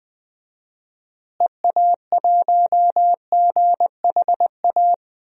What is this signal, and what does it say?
Morse code, heard as a keyed tone, about 20 wpm, 700 Hz. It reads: EA1GHA
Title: DXCC-100 Call Signs - 6 Characters - Repeated 3x - 20wpm